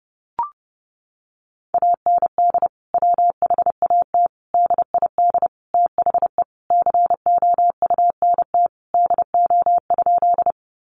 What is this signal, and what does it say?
Morse code, heard as a keyed tone, about 28 wpm, 700 Hz. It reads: AND WHAT DID THE COUNT DO?
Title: Sentences from Top 1500 Words 30wpm